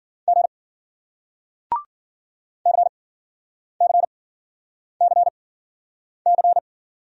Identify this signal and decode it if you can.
Morse code, heard as a keyed tone, about 48 wpm, spaced as for 9 wpm, 700 Hz. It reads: G CCCC
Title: Speed-Racing Single Letters 40wpm